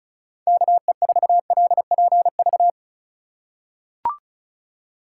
Morse code, heard as a keyed tone, about 34 wpm, 690 Hz. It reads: KE4LPV
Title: USA Call Signs - 6 Characters - 4x Character Spacing - 35wpm